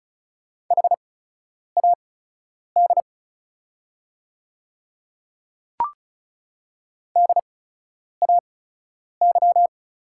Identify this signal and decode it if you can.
Morse code, heard as a keyed tone, about 36 wpm, spaced as for 10 wpm, 690 Hz. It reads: HAD DAY
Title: 8x Character Spacing - Top 100 Words - 35wpm